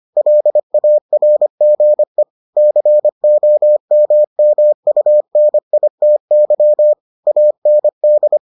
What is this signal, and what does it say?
Morse code, heard as a keyed tone, about 25 wpm, 600 Hz. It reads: LARGE COMMUNITY AND